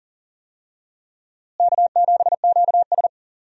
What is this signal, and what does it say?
Morse code, heard as a keyed tone, about 40 wpm, 700 Hz. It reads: K7QS
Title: USA Call Signs - 4 Characters - Repeated 3x - 40wpm